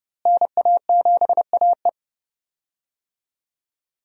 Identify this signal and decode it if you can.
Morse code, heard as a keyed tone, about 30 wpm, 700 Hz. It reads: NA7AE
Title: USA Call Signs - 5 Characters - Repeated 2x - 30wpm